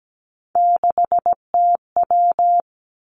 Morse code, heard as a keyed tone, about 17 wpm, 700 Hz. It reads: 6TW
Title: Three Challenging Letter-Number Combinations 17wpm